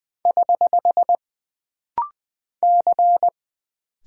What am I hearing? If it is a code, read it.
Morse code, heard as a keyed tone, about 20 wpm, 700 Hz. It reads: <HH> C
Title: Single Character (Letter:Number:Punctuation:Prosign) - Rapid-Fire - 20wpm